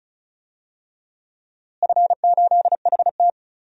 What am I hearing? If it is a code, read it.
Morse code, heard as a keyed tone, about 34 wpm, 700 Hz. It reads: F8HT
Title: DXCC-100 Call Signs - 4 Characters - Repeated 2x - 35wpm